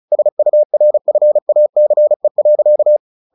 Morse code, heard as a keyed tone, about 34 wpm, 600 Hz. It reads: SURFACE.
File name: the old man and the sea / the old man and the sea 1 35wpm